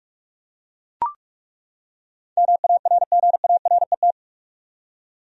Morse code, heard as a keyed tone, about 42 wpm, 700 Hz. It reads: MARGARET
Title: USA Common Names - 45wpm